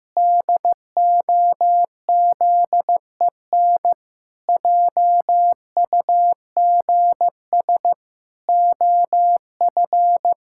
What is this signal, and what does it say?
Morse code, heard as a keyed tone, about 15 wpm, 700 Hz. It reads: DOZEN JUGS OF
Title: Pangram Challenge v2 - 15wpm